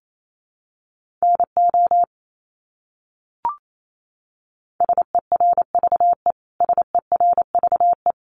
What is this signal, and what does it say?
Morse code, heard as a keyed tone, about 28 wpm, 700 Hz. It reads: NO SERVE SERVE
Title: Top 500 Words Repeated 28wpm